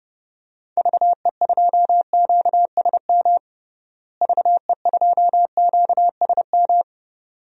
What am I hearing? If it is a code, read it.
Morse code, heard as a keyed tone, about 32 wpm, 700 Hz. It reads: VE2QSM VE2QSM
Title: DXCC-100 Call Signs - 6 Characters - Repeated 2x - 30wpm